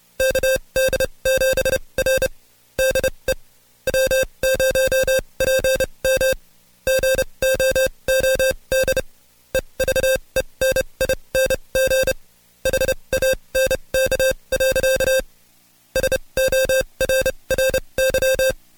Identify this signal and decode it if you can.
Morse code, heard as a keyed tone, about 29 wpm, spaced as for 23 wpm, 530 Hz. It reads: KD7R DE W0PM GOOD EVENING HANK. SORRY
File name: Code Quick / AudioDownld2 / Seg138-Stepper11 23WPM